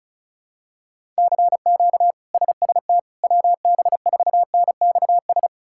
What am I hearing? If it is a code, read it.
Morse code, heard as a keyed tone, about 36 wpm, 700 Hz. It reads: CQ SST WB4NXS
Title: K1USN SST QSO Practice - 35wpm